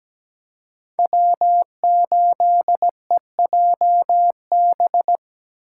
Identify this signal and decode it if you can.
Morse code, heard as a keyed tone, about 20 wpm, 700 Hz. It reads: W8EJB
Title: Speed Racing - USA 1x3 Call Signs 17wpm